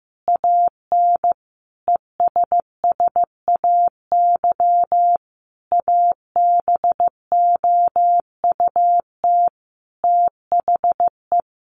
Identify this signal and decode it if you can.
Morse code, heard as a keyed tone, about 15 wpm, 700 Hz. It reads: AN ESSAY ABOUT THE
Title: Sentences from Top 1500 Words 15wpm